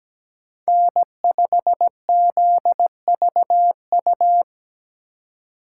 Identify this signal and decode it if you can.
Morse code, heard as a keyed tone, about 20 wpm, 700 Hz. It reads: N5ZVU
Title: Speed Racing - USA 1x3 Call Signs 17wpm